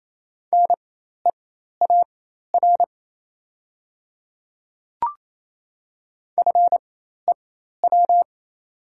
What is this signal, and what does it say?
Morse code, heard as a keyed tone, about 28 wpm, spaced as for 13 wpm, 700 Hz. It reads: NEAR FEW
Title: 4x Character Spacing - Top 300 Words - 28wpm